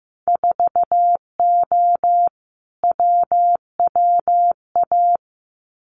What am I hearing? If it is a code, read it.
Morse code, heard as a keyed tone, about 15 wpm, 700 Hz. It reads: 4O WWA
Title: Winter Field Day Exchanges 15wpm